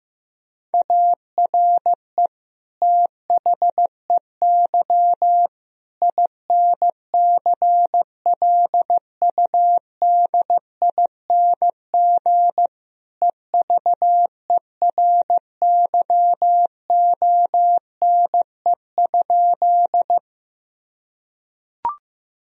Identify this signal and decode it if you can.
Morse code, heard as a keyed tone, about 15 wpm, 700 Hz. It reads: ARE THEY INCLUDING EVERYONE?
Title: Sentences from Top 1000 Words - Review Part 4 - 15wpm